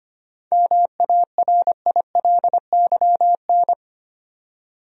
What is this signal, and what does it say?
Morse code, heard as a keyed tone, about 25 wpm, 700 Hz. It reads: MARILYN
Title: USA Common Names - 25wpm